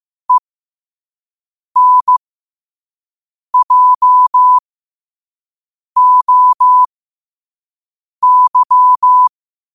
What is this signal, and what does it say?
Morse code, heard as a keyed tone, about 15 wpm, spaced as for 5 wpm, 1000 Hz. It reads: ENJOY
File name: qotdfiles / 5wpm